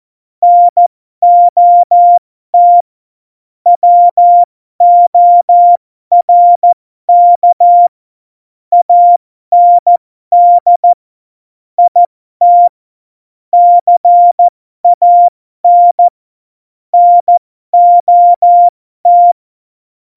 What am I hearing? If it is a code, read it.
Morse code, heard as a keyed tone, about 14 wpm, spaced as for 12 wpm, 700 Hz. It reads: NOT WORK AND IT CAN NOT
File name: Morsekurs / Witze